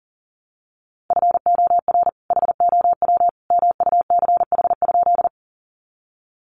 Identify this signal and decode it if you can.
Morse code, heard as a keyed tone, about 40 wpm, 700 Hz. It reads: FOR HOW MUCH?
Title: Sentences from Top 200 Words 40wpm